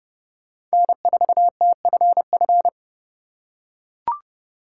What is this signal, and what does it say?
Morse code, heard as a keyed tone, about 30 wpm, 700 Hz. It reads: N4TFF
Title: USA Call Signs - 5 Characters - 2x Character Spacing - 30wpm